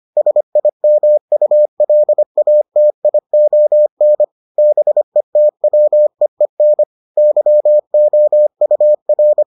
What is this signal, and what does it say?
Morse code, heard as a keyed tone, about 25 wpm, 600 Hz. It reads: SIMULATION BETWEEN YOUR